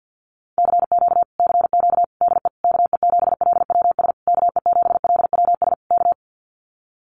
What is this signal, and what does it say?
Morse code, heard as a keyed tone, about 49 wpm, 700 Hz. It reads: CQ CQ DE KE7LWS KE7LWS K
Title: Common QSO Phrases Encore 50wpm